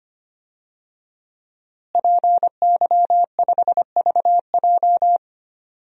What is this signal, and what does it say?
Morse code, heard as a keyed tone, about 25 wpm, 700 Hz. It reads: PY5VJ
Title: DXCC-100 Call Signs - 5 Characters - Repeated 3x - 25wpm